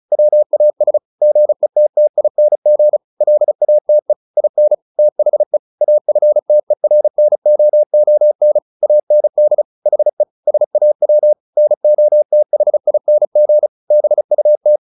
Morse code, heard as a keyed tone, about 34 wpm, 600 Hz. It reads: WAS GETTING LATE IN THE AFTERNOON AND HE SAW NOTHING BUT
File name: the old man and the sea 2 35wpm